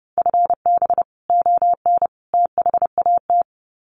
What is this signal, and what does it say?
Morse code, heard as a keyed tone, about 30 wpm, 700 Hz. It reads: FB ON THAT
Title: Common QSO Phrases 30wpm